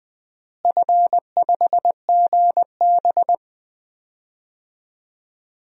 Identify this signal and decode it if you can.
Morse code, heard as a keyed tone, about 19 wpm, 690 Hz. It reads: F5GB
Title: DXCC-100 Call Signs - 4 Characters - Repeated 3x - 20wpm